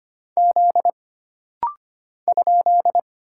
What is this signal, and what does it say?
Morse code, heard as a keyed tone, about 25 wpm, 700 Hz. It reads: Z ?
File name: Single Character (Letter:Number:Punctuation:Prosign) - Rapid-Fire - 25wpm